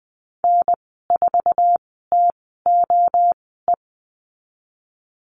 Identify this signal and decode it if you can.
Morse code, heard as a keyed tone, about 20 wpm, spaced as for 14 wpm, 700 Hz. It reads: N4TOE
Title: 069 Lesson 12g - Letter d - Call Signs 20wpm